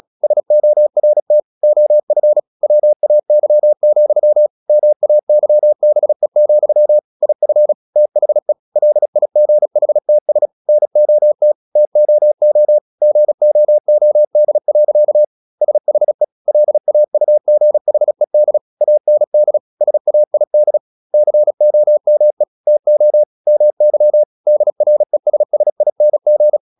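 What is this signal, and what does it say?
Morse code, heard as a keyed tone, about 37 wpm, 600 Hz. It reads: SORT OF WAY, MAYBE, IF THE LIGHTS NOT TOO GOOD. SHE LAUGHED AND SAID COME TO MY DRESSING
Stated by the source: Morse practice at 36 wpm, text of Ken Follett's novel Never, from CW med Gnister.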